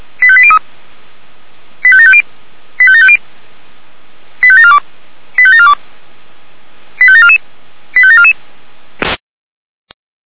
Five-tone selective call (ZVEI), heard as quick runs of five tones, tones of about 70 ms. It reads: 85803 85750 85750 85742 85742 85744 85744